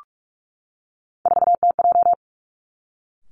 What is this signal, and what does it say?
Morse code, heard as a keyed tone, about 46 wpm, 700 Hz. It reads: 4TJ
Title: Three Challenging Letter-Number Combinations 45wpm